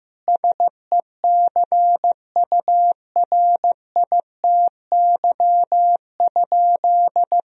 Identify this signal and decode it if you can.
Morse code, heard as a keyed tone, about 15 wpm, 700 Hz. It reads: SECURITY?
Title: Sentences from Top 1000 Words - Review Part 4 - 15wpm